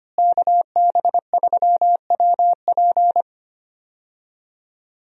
Morse code, heard as a keyed tone, about 25 wpm, 700 Hz. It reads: KB3WP
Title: USA Call Signs - 5 Characters - Repeated 2x - 25wpm